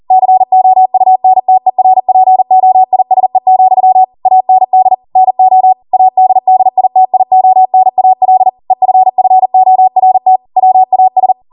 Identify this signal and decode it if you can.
Morse code, heard as a keyed tone, about 40 wpm, 750 Hz. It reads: COUNTERPOISE, AND NO ADDITIONAL EFFORT WAS